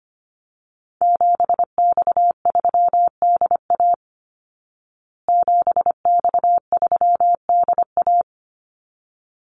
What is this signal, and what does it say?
Morse code, heard as a keyed tone, about 25 wpm, 700 Hz. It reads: 7X3DA 7X3DA
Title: Call Signs Repeated 3x - 25wpm